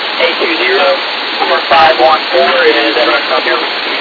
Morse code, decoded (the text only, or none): none